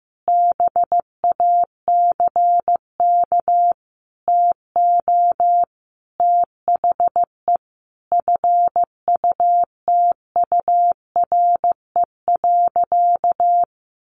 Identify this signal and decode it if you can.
Morse code, heard as a keyed tone, about 15 wpm, 700 Hz. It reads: BACK TO THE FUTURE.
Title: A Sound Of Thunder 15wpm